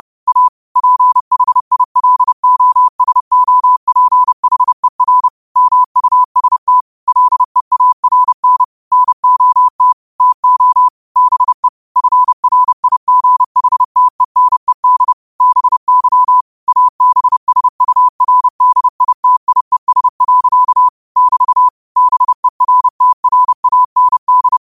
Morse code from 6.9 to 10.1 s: LEARN NOT

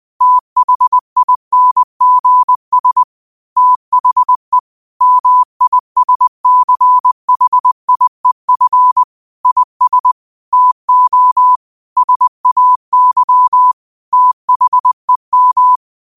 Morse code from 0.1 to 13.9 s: THINGS THE MISCHIEF IS TO SAY

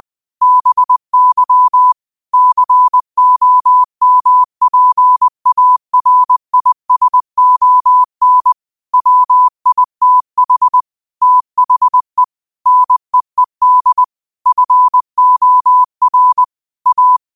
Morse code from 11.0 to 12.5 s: THE

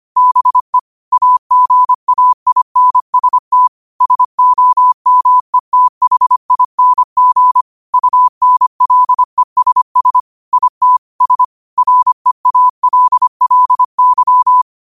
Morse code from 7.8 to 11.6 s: UNLESS IT S

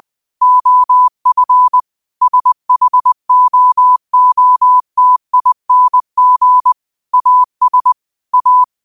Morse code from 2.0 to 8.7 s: SHOOTING AS A

